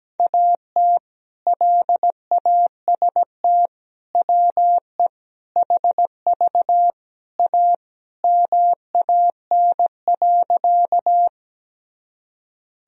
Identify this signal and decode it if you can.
Morse code, heard as a keyed tone, about 17 wpm, 700 Hz. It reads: AT LAST WE HV A MAN.
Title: Sentences from Top 300 Words and Abbreviations 17wpm